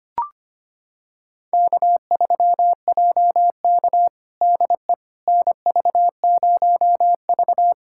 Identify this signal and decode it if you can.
Morse code, heard as a keyed tone, about 25 wpm, 700 Hz. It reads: K3JK DE NV0V
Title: USA Callsign DE Callsign - 4 Characters - 25wpm